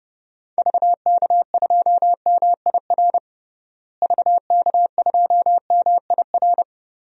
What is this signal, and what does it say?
Morse code, heard as a keyed tone, about 31 wpm, 700 Hz. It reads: VK2MIR VK2MIR
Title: DXCC-100 Call Signs - 6 Characters - Repeated 2x - 30wpm